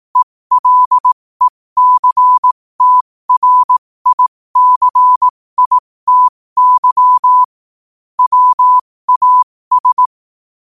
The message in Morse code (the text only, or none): ELECTRICITY WAS